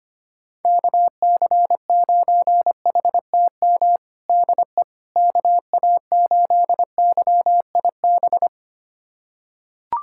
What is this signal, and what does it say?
Morse code, heard as a keyed tone, about 25 wpm, 700 Hz. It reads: KC9HTM DE KA8YIB
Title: USA Callsign DE Callsign 25wpm